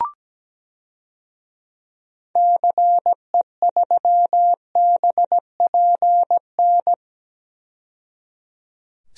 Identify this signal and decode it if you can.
Morse code, heard as a keyed tone, about 17 wpm, 700 Hz. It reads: CE3BPN
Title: DXCC-100 Call Signs - 6 Characters - 17wpm